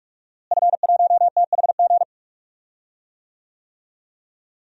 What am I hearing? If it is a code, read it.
Morse code, heard as a keyed tone, about 46 wpm, 700 Hz. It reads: F1THG
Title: DXCC-100 Call Signs - 5 Characters - Repeated 3x - 45wpm